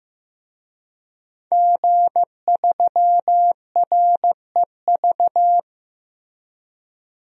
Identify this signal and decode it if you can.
Morse code, heard as a keyed tone, about 15 wpm, 700 Hz. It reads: G3REV
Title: DXCC-100 Call Signs - 5 Characters - Repeated 3x - 15wpm